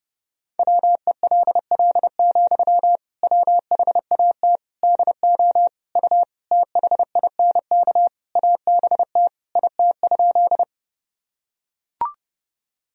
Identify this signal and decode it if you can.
Morse code, heard as a keyed tone, about 30 wpm, 700 Hz. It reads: WELL, WHAT DO U THINK ABT IT?
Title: Sentences from Top 200 Words and Abbreviations 30wpm